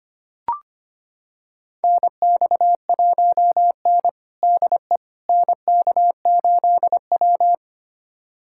Morse code, heard as a keyed tone, about 25 wpm, 700 Hz. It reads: NX1N DE NK8W
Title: USA Callsign DE Callsign - 4 Characters - 25wpm